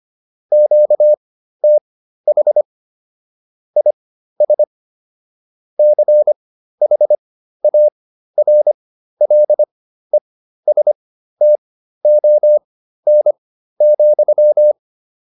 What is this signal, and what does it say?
Morse code, heard as a keyed tone, about 25 wpm, spaced as for 13 wpm, 600 Hz. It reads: QTH IS CHARLESTON,
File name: QSO303 13